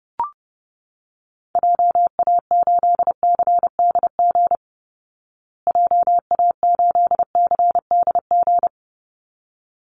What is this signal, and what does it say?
Morse code, heard as a keyed tone, about 30 wpm, 700 Hz. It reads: JA8CDG JA8CDG
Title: Call Signs Repeated 3x - 30wpm